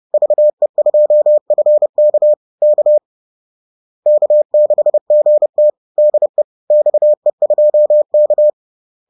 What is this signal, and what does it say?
Morse code, heard as a keyed tone, about 30 wpm, 600 Hz. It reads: VE2FK K K6GT DE XE2K
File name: 30wpm-4